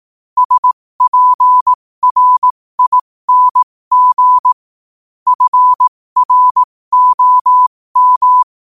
Morse code, heard as a keyed tone, about 18 wpm, spaced as for 15 wpm, 1000 Hz. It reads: SPRING FROM
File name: qotdfiles / 15wpm